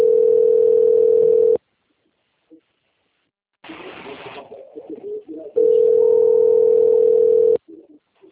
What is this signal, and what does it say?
Ringback tone (North America), 440 + 480 Hz.
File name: out- - - - - .19038